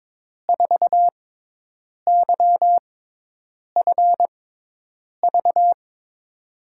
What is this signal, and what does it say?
Morse code, heard as a keyed tone, about 22 wpm, spaced as for 8 wpm, 700 Hz. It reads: 4YFV
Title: USA Call Signs - 6 Characters - 6x Character Spacing - 22wpm